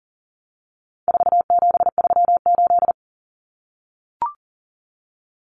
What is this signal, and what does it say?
Morse code, heard as a keyed tone, about 37 wpm, 700 Hz. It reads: 4738
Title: Numbers - 4-Digit 40wpm